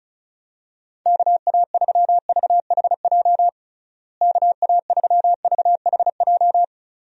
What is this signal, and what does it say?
Morse code, heard as a keyed tone, about 36 wpm, 700 Hz. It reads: KA3VHJ KA3VHJ
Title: USA Call Signs - 6 Characters - Repeated 2x - 35wpm